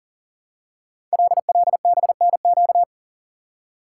Morse code, heard as a keyed tone, about 42 wpm, 690 Hz. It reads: LLBNQ
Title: Code Groups - 5 Letters - 40wpm